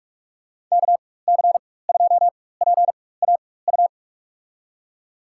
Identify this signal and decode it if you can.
Morse code, heard as a keyed tone, about 42 wpm, spaced as for 20 wpm, 700 Hz. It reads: KC2PAU
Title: USA Call Signs - 6 Characters - 4x Character Spacing - 45wpm